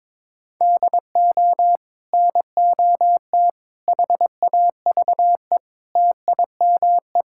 Morse code, heard as a keyed tone, about 22 wpm, 700 Hz. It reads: DO NOT HAVE TIME
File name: practice / 22wpm / Build A Phrase - 5-Word Phrases - 22wpm